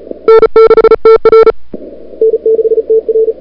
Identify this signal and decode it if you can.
Morse code, heard as a keyed tone, about 36 wpm, 440 Hz. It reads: N6TR N6TR
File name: n6tr / Fab2007CWSprintHour1